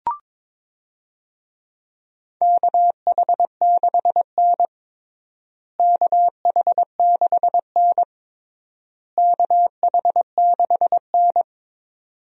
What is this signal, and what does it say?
Morse code, heard as a keyed tone, about 22 wpm, 700 Hz. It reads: KH6N KH6N KH6N
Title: DXCC-100 Call Signs - 4 Characters - Repeated 3x - 22wpm